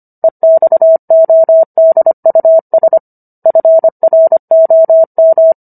Morse code, heard as a keyed tone, about 25 wpm, 650 Hz. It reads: EXODUS FROM